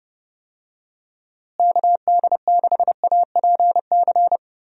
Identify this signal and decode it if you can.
Morse code, heard as a keyed tone, about 30 wpm, 700 Hz. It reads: KD6APC
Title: USA Call Signs - 6 Characters - Repeated 2x - 30wpm